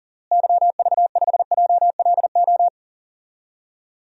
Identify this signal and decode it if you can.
Morse code, heard as a keyed tone, about 40 wpm, 700 Hz. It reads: YV5JLO
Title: DXCC-100 Call Signs - 6 Characters - Repeated 3x - 40wpm